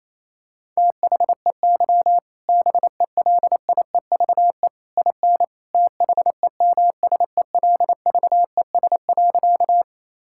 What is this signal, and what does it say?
Morse code, heard as a keyed tone, about 28 wpm, 700 Hz. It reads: THEY BELIEVE IN THEMSELVES.
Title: Sentences from Top 1000 Words - Review Part 5 - 28wpm